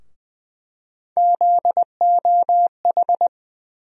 Morse code, heard as a keyed tone, about 20 wpm, 700 Hz. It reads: ZOH
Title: USA Call Sign Postfixes - 3 Letters 20wpm